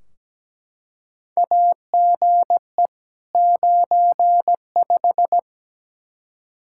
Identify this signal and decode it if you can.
Morse code, heard as a keyed tone, about 17 wpm, 700 Hz. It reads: AGE 95
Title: K1USN SST QSO Practice 17wpm